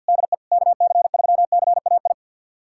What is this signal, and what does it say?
Morse code, heard as a keyed tone, about 53 wpm, 700 Hz. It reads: DE KC3CAI
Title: USA Callsign DE Callsign - 6 Characters - 50wpm